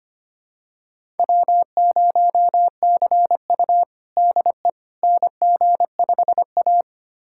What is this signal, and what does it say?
Morse code, heard as a keyed tone, about 25 wpm, 700 Hz. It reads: W0CU DE NG5A
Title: USA Callsign DE Callsign - 4 Characters - 25wpm